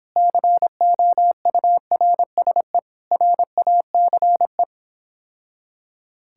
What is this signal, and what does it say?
Morse code, heard as a keyed tone, about 26 wpm, 700 Hz. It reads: COURSE RACE